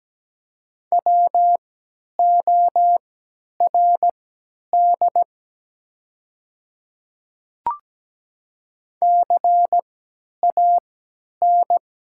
Morse code, heard as a keyed tone, about 17 wpm, spaced as for 10 wpm, 700 Hz. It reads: WORD CAN
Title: 3x Character Spacing - Top 100 Words - 17wpm